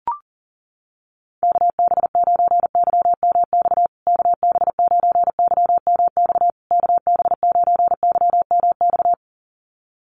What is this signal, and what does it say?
Morse code, heard as a keyed tone, about 40 wpm, 700 Hz. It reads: KB9YMX KB9YMX KB9YMX
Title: USA Call Signs Repeated 3x 40wpm